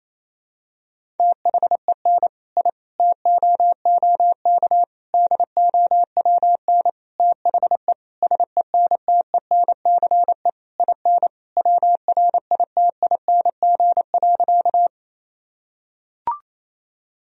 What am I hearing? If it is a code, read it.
Morse code, heard as a keyed tone, about 28 wpm, 700 Hz. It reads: THEN I TOOK DOWN THE SENTENCE IN WRITING.
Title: Sentences from Top 1000 Words - Review Part 1 - 28wpm